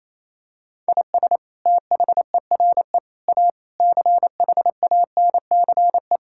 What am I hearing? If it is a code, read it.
Morse code, heard as a keyed tone, about 28 wpm, 690 Hz. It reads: IS THERE A CHANCE
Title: Sentences from Top 1000 Words - Review Part 6 - 28wpm